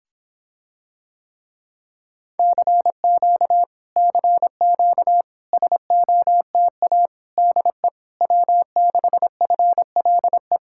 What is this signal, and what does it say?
Morse code, heard as a keyed tone, about 26 wpm, 700 Hz. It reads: CQ CQ SOTA DE W6FLE